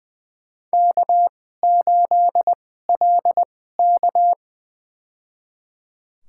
Morse code, heard as a keyed tone, about 20 wpm, spaced as for 15 wpm, 700 Hz. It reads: K8LK
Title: USA Call Signs - 4 Characters - 2x Character Spacing - 20wpm